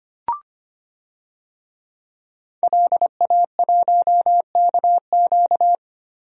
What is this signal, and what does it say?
Morse code, heard as a keyed tone, about 25 wpm, 700 Hz. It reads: LA1KQ
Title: DXCC-100 Call Signs - 5 Characters - Repeated 3x - 25wpm